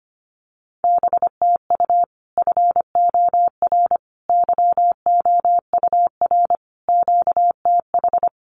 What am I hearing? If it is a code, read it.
Morse code, heard as a keyed tone, about 25 wpm, 700 Hz. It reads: BTU FOR YOUR QTH
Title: Common QSO Phrases 25wpm